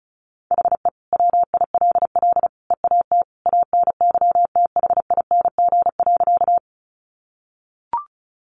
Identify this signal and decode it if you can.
Morse code, heard as a keyed tone, about 34 wpm, 700 Hz. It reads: HE WILL EAT ANYTHING.
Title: Sentences from Top 1000 Words 35wpm